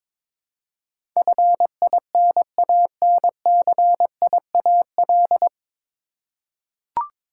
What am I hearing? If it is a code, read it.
Morse code, heard as a keyed tone, about 22 wpm, 700 Hz. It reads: FINANCIAL
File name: 9 Letter Words - First 3 Letters Given - 22wpm